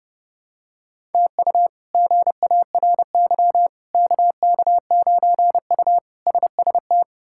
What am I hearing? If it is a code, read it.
Morse code, heard as a keyed tone, about 30 wpm, 700 Hz. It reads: TU GARY KK9U SST
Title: K1USN SST QSO Practice - 30wpm